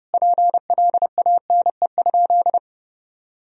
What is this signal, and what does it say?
Morse code, heard as a keyed tone, about 30 wpm, 700 Hz. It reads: PLANE?
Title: Sentences from Top 1000 Words - Review Part 3 - 30wpm